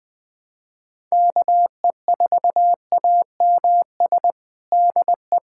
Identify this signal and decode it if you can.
Morse code, heard as a keyed tone, about 20 wpm, 700 Hz. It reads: KE4AMS DE
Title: USA Callsign DE Callsign 20wpm